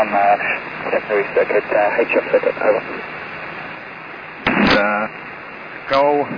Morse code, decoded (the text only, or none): none